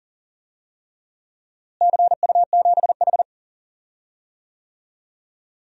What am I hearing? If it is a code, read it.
Morse code, heard as a keyed tone, about 40 wpm, 700 Hz. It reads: CU7H
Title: DXCC-100 Call Signs - 4 Characters - Repeated 3x - 40wpm